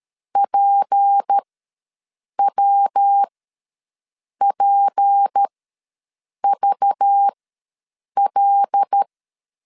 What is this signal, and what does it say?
Morse code, heard as a keyed tone, about 13 wpm, spaced as for 6 wpm, 800 Hz. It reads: PWPVL
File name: LEZIONI / 6 PKX0 / 6 P ws 12wpm 3117